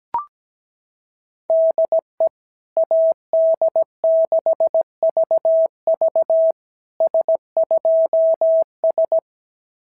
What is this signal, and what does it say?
Morse code, heard as a keyed tone, about 17 wpm, 690 Hz. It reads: DE AD6VV S2S
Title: SOTA QSO Practice 17wpm